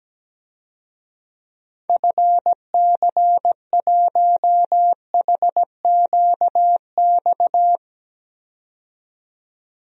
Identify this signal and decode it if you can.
Morse code, heard as a keyed tone, about 17 wpm, 700 Hz. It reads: FC1HQX
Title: DXCC-100 Call Signs - 6 Characters - 17wpm